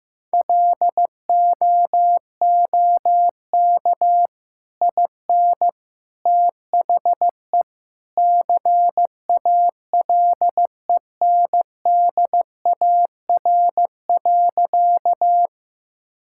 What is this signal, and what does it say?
Morse code, heard as a keyed tone, about 14 wpm, 700 Hz. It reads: LOOK IN THE CALENDAR.
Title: Sentences from Top 1600 Words 15wpm